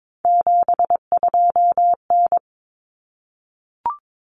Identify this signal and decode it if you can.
Morse code, heard as a keyed tone, about 22 wpm, 700 Hz. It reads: 72N
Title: Common Cut Numbers - 3-Digit 22wpm